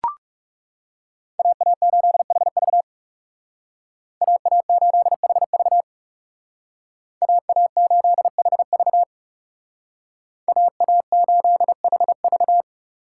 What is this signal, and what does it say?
Morse code, heard as a keyed tone, about 36 wpm, 700 Hz. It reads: AA8HV AA8HV AA8HV AA8HV
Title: Speed Racing - USA 2x2 Call Signs 30wpm